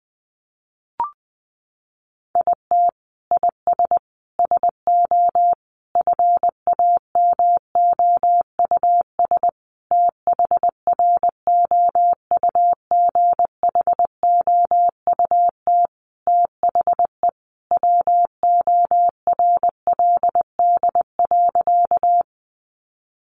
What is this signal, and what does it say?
Morse code, heard as a keyed tone, about 20 wpm, 700 Hz. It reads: IT IS SO FAMOUS THROUGHOUT THE WORLD.
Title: Sentences from Top 1500 Words 20wpm